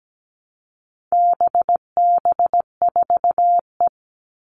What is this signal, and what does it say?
Morse code, heard as a keyed tone, about 17 wpm, 700 Hz. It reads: BB4E
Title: All Cut Numbers - 4-Digit 17wpm